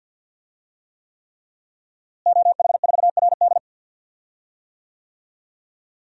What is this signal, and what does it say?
Morse code, heard as a keyed tone, about 52 wpm, 700 Hz. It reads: OH4RD
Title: DXCC-100 Call Signs - 5 Characters - Repeated 3x - 50wpm